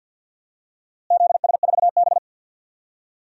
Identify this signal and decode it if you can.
Morse code, heard as a keyed tone, about 51 wpm, 700 Hz. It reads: ZS4B
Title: DXCC-100 Call Signs - 4 Characters - Repeated 2x - 50wpm